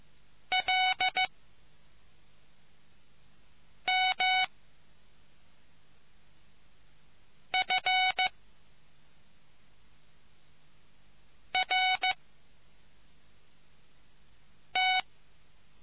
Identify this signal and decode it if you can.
Morse code, heard as a keyed tone, about 15 wpm, spaced as for 3 wpm, 740 Hz. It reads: LMFRT